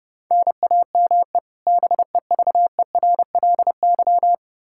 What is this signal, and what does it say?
Morse code, heard as a keyed tone, about 32 wpm, 700 Hz. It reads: NAME BEVERLY
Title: Practice QSO - Exchange 1 - Beginning - 30wpm